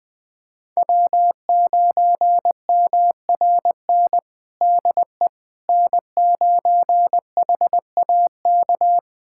Morse code, heard as a keyed tone, about 20 wpm, 700 Hz. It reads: W9MRN DE N9HAK
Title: USA Callsign DE Callsign - 5 Characters - 20wpm